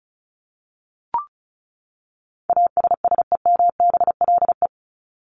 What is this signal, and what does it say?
Morse code, heard as a keyed tone, about 34 wpm, 700 Hz. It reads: ASSEMBLE
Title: LY Suffix - Top 100 Words with Base Words 35wpm